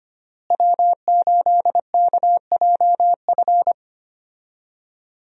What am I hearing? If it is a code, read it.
Morse code, heard as a keyed tone, about 25 wpm, 700 Hz. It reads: W8KJF
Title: USA Call Signs - 5 Characters - Repeated 3x - 25wpm